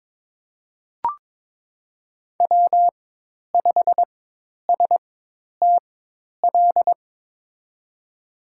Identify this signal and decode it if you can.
Morse code, heard as a keyed tone, about 23 wpm, spaced as for 10 wpm, 700 Hz. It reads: W5STL
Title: USA Call Signs - 5 Characters - 4x Character Spacing - 22wpm